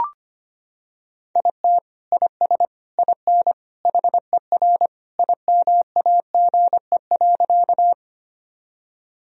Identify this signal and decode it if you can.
Morse code, heard as a keyed tone, about 25 wpm, 700 Hz. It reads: IT IS IN HER IMAGE.
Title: Sentences from Top 1000 Words - Review Part 4 - 25wpm